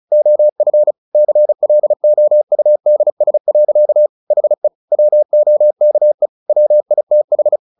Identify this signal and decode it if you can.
Morse code, heard as a keyed tone, about 34 wpm, 600 Hz. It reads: OF CLOUDS. HE WOKE WITH
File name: the old man and the sea 2 35wpm